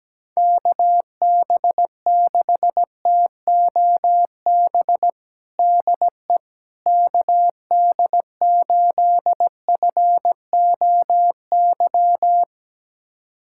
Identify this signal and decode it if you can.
Morse code, heard as a keyed tone, about 17 wpm, 700 Hz. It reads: KB6TOB DE KD8FOY
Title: USA Callsign DE Callsign 17wpm